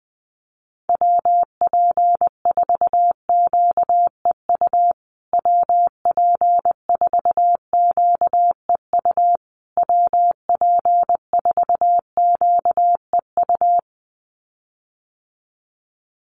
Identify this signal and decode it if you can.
Morse code, heard as a keyed tone, about 19 wpm, 700 Hz. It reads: WP4QEU WP4QEU WP4QEU
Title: USA Call Signs Repeated 3x 20wpm